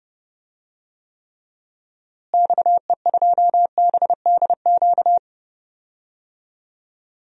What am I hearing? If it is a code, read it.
Morse code, heard as a keyed tone, about 32 wpm, 700 Hz. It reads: XE2BDQ
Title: DXCC-100 Call Signs - 6 Characters - Repeated 2x - 30wpm